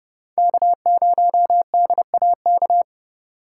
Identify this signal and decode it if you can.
Morse code, heard as a keyed tone, about 38 wpm, 700 Hz. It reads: K0DAK